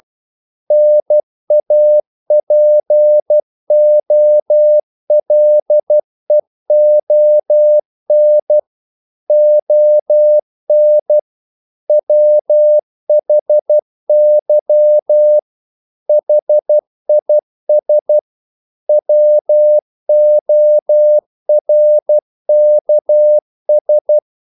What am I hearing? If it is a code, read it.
Morse code, heard as a keyed tone, about 12 wpm, 600 Hz. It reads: NAPOLEON ON WHY HIS WORKS
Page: English quotations in CW (telegraphy - morse)